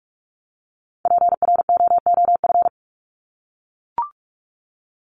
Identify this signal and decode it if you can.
Morse code, heard as a keyed tone, about 44 wpm, 700 Hz. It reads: PROOF
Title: Top 1501-1750 Words 45wpm